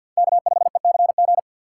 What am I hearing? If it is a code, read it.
Morse code, heard as a keyed tone, about 52 wpm, 700 Hz. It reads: K5ECG
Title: USA Call Signs - 5 Characters - 6x Character Spacing - 50wpm